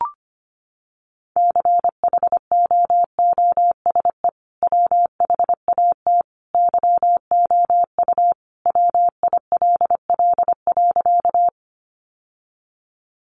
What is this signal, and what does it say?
Morse code, heard as a keyed tone, about 25 wpm, 700 Hz. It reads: CHOOSE WHAT YOU WILL.
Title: Sentences from Top 700 Words 25wpm